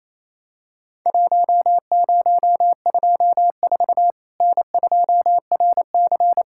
Morse code, heard as a keyed tone, about 28 wpm, 700 Hz. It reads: 1024 N2RC
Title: Worked All Europe DX Contest - QTC Exchange - 28wpm